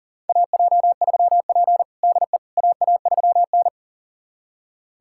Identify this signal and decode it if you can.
Morse code, heard as a keyed tone, about 40 wpm, 700 Hz. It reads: AJ3P DE AA3N
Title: USA Callsign DE Callsign - 4 Characters - 40wpm